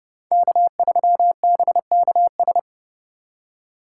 Morse code, heard as a keyed tone, about 36 wpm, 700 Hz. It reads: K3BKS
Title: Speed Racing - USA 1x3 Call Signs 30wpm